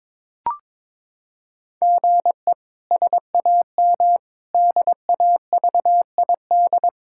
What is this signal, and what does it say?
Morse code, heard as a keyed tone, about 22 wpm, 700 Hz. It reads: GE SAM DAVID
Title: K1USN SST QSO Practice - 22wpm